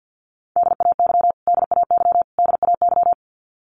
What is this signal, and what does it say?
Morse code, heard as a keyed tone, about 50 wpm, 700 Hz. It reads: DAY DAY DAY
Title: Top 500 Words Repeated 50wpm